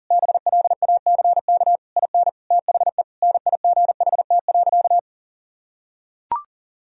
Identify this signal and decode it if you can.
Morse code, heard as a keyed tone, about 40 wpm, 700 Hz. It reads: BLACK IN THE NIGHT.
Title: Sentences from Top 1000 Words - Review Part 4 - 40wpm